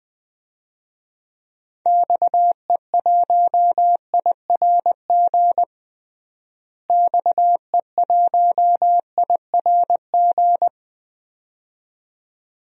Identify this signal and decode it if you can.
Morse code, heard as a keyed tone, about 20 wpm, 700 Hz. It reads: XE1IRG XE1IRG
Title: DXCC-100 Call Signs - 6 Characters - Repeated 2x - 20wpm